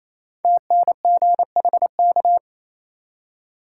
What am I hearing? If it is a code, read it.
Morse code, heard as a keyed tone, about 28 wpm, 700 Hz. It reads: TNGHK
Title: Code Groups - 5 Letter:Numbers - 28wpm